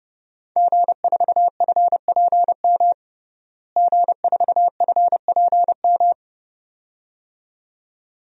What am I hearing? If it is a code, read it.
Morse code, heard as a keyed tone, about 32 wpm, 700 Hz. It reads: G4FPM G4FPM
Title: DXCC-100 Call Signs - 5 Characters - Repeated 2x - 30wpm